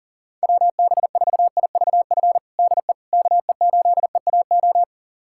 Morse code, heard as a keyed tone, about 40 wpm, 700 Hz. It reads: WB4IVF DE KE8EAO
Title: USA Callsign DE Callsign 40wpm